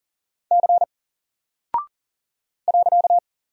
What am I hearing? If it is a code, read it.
Morse code, heard as a keyed tone, about 40 wpm, 700 Hz. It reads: C .
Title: Single Character (Letter:Number:Punctuation:Prosign) Alt 40wpm